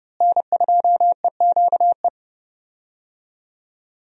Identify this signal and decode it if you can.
Morse code, heard as a keyed tone, about 30 wpm, 700 Hz. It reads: N2EQE
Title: USA Call Signs - 5 Characters - Repeated 2x - 30wpm